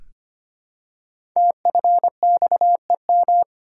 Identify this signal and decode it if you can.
Morse code, heard as a keyed tone, about 25 wpm, 700 Hz. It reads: TFXEM
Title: Code Groups - 5 Letters - 25wpm